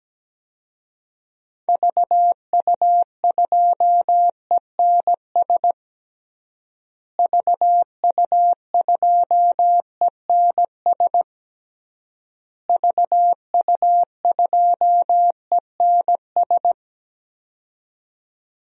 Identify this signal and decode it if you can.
Morse code, heard as a keyed tone, about 17 wpm, 700 Hz. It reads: VU2ENS VU2ENS VU2ENS
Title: DXCC-100 Call Signs - 6 Characters - Repeated 3x - 17wpm